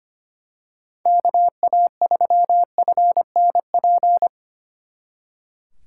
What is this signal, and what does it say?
Morse code, heard as a keyed tone, about 29 wpm, 700 Hz. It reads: KA3FNP
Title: Speed Racing - USA 2x3 Call Signs 25wpm